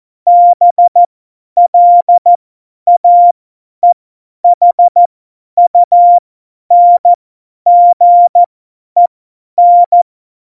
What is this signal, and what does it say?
Morse code, heard as a keyed tone, about 14 wpm, spaced as for 10 wpm, 700 Hz. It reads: BLAEHUNGEN
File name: Morsekurs / Witze / Witz 28. Mai 2019, 50 F70